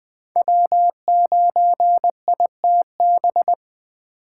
Morse code, heard as a keyed tone, about 20 wpm, 700 Hz. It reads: W9ITB
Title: USA Call Signs - 5 Characters - 2x Character Spacing - 20wpm